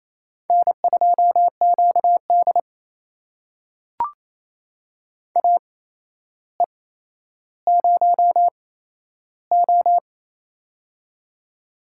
Morse code, heard as a keyed tone, about 28 wpm, spaced as for 12 wpm, 700 Hz. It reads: N2QD AE0O
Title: USA Call Signs - 4 Characters - 8x Character Spacing - 28wpm